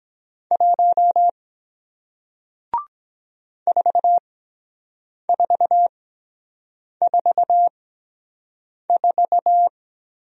Morse code, heard as a keyed tone, about 23 wpm, spaced as for 7 wpm, 700 Hz. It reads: 1 4444